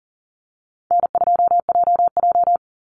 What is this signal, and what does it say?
Morse code, heard as a keyed tone, about 40 wpm, 700 Hz. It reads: N2JJ
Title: Easy Call Signs Repeated 2x - 40wpm